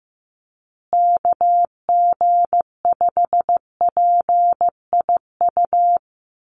Morse code, heard as a keyed tone, about 15 wpm, 700 Hz. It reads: KG5PIU
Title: USA Call Signs Repeated 3x 15wpm